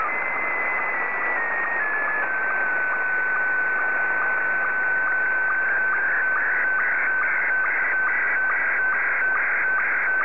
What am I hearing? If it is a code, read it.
SSTV (slow-scan television), Scottie 1.
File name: sstv / S1-VK7ZGK